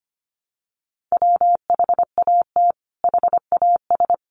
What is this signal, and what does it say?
Morse code, heard as a keyed tone, about 25 wpm, 700 Hz. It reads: WHAT HAS
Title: Sentences from Top 500 Words 25wpm